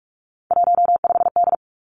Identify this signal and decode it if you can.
Morse code, heard as a keyed tone, about 46 wpm, 700 Hz. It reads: 15D
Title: Three Challenging Letter-Number Combinations 45wpm